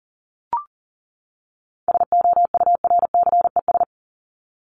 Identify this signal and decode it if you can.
Morse code, heard as a keyed tone, about 40 wpm, 700 Hz. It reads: SOURCES
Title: ES:S Suffix - Top 100 Words with Base Words 40wpm